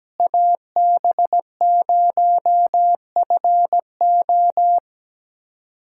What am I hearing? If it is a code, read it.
Morse code, heard as a keyed tone, about 17 wpm, 700 Hz. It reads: AB0FO
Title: USA Callsign DE Callsign - 5 Characters - 17wpm